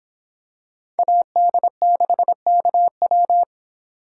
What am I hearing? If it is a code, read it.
Morse code, heard as a keyed tone, about 26 wpm, 700 Hz. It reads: AD6KW